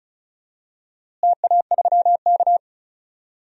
Morse code, heard as a keyed tone, about 36 wpm, 700 Hz. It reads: TA3K
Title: DXCC-100 Call Signs - 4 Characters - Repeated 3x - 35wpm